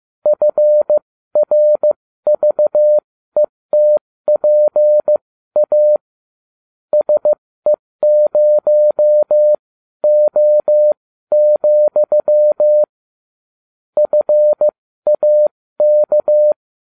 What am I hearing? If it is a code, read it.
Morse code, heard as a keyed tone, about 15 wpm, spaced as for 13 wpm, 600 Hz. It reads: FRVETPA SE0O, FAK